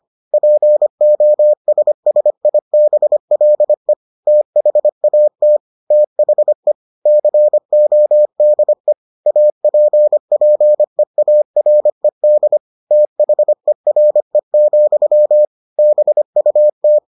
Morse code, heard as a keyed tone, about 25 wpm, 600 Hz. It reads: POSSIBLE THAT THE CODE APPEARED THERE, BUT